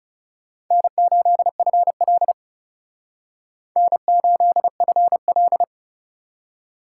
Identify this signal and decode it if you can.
Morse code, heard as a keyed tone, about 36 wpm, 700 Hz. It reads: N8FL N8FL
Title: Speed Racing - USA Call Signs 30wpm